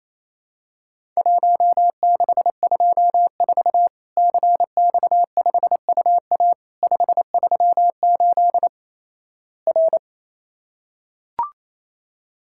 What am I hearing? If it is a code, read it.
Morse code, heard as a keyed tone, about 28 wpm, 700 Hz. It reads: 1624 CX5UA 538 R
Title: Worked All Europe DX Contest - QTC Exchange - 28wpm